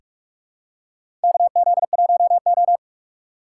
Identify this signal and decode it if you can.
Morse code, heard as a keyed tone, about 44 wpm, 700 Hz. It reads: KZ1O
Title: USA Call Signs - 4 Characters - 6x Character Spacing - 45wpm